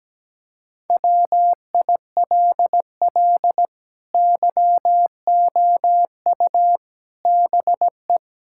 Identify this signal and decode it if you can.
Morse code, heard as a keyed tone, about 17 wpm, 700 Hz. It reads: WILL YOU BE
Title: Sentences from Top 1600 Words 17wpm